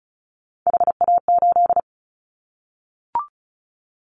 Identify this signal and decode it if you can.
Morse code, heard as a keyed tone, about 36 wpm, 700 Hz. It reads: HA8
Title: Callsign Prefixes for Top 50 DXCC - 35wpm